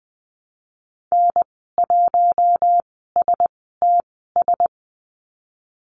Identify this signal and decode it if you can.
Morse code, heard as a keyed tone, about 20 wpm, spaced as for 14 wpm, 700 Hz. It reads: N1STS
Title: 076 Lesson 13g - Letter L - Call Signs 20wpm